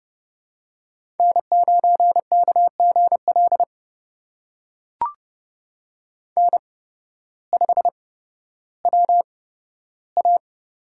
Morse code, heard as a keyed tone, about 29 wpm, spaced as for 14 wpm, 700 Hz. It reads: N9KGL N5WA
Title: USA Call Signs - 5 Characters - 8x Character Spacing - 30wpm